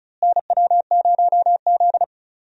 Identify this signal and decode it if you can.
Morse code, heard as a keyed tone, about 36 wpm, 700 Hz. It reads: NW0Z
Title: USA Call Signs - 4 Characters - Repeated 3x - 35wpm